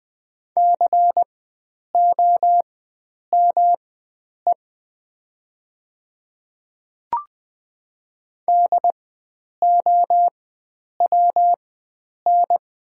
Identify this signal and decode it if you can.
Morse code, heard as a keyed tone, about 20 wpm, spaced as for 9 wpm, 700 Hz. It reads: COME DOWN
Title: 4x Character Spacing - Top 100 Words - 20wpm